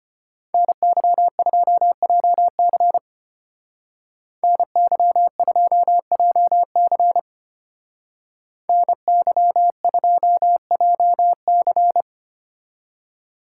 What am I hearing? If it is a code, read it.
Morse code, heard as a keyed tone, about 31 wpm, 700 Hz. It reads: NY2JC NY2JC NY2JC
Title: Speed Racing - USA Call Signs 25wpm